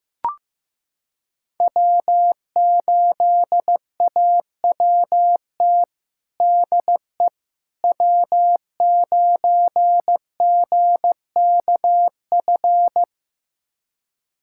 Morse code, heard as a keyed tone, about 15 wpm, 700 Hz. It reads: W8AWT DE W9GKF
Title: USA Callsign DE Callsign - 5 Characters - 15wpm